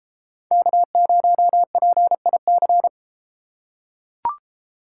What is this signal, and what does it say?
Morse code, heard as a keyed tone, about 31 wpm, 700 Hz. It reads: K0PIC